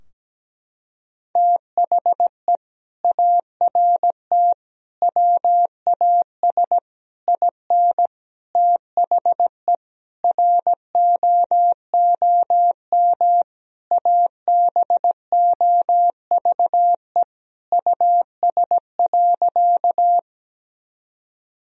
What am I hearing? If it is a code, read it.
Morse code, heard as a keyed tone, about 17 wpm, 700 Hz. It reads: THE ART WAS IN THE ROOM ABOVE US.
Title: Sentences from Top 1000 Words - Review Part 3 - 17wpm